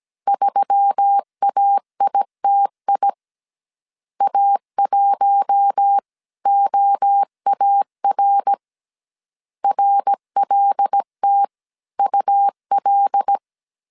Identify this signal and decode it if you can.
Morse code, heard as a keyed tone, about 17 wpm, 800 Hz. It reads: 3AITI A1OAR RLTUL
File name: CW / DET / 1 6 1500C S A 16wpm 2635